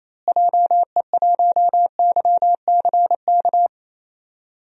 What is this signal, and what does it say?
Morse code, heard as a keyed tone, about 28 wpm, 700 Hz. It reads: JE1YCK
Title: DXCC-100 Call Signs - 6 Characters - Repeated 3x - 28wpm